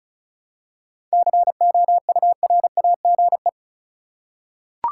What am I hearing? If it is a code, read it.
Morse code, heard as a keyed tone, about 34 wpm, 700 Hz. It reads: COURAGE
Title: Seven Letter Words - 35wpm